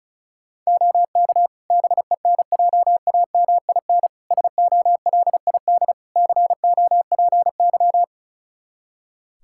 Morse code, heard as a keyed tone, about 36 wpm, 700 Hz. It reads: OK BENJAMIN SOLID COPY
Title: Practice QSO - Exchange 3 - More Info - 35wpm